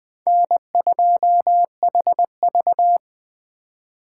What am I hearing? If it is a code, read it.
Morse code, heard as a keyed tone, about 21 wpm, 700 Hz. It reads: N2HV